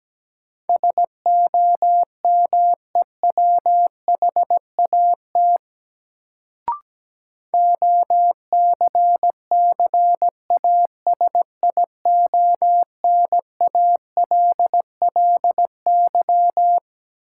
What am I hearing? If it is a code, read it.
Morse code, heard as a keyed tone, about 17 wpm, 700 Hz. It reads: SOMEWHAT OCCASIONALLY
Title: Sentences from Top 1300 Words - New Word Drill Alt - 17wpm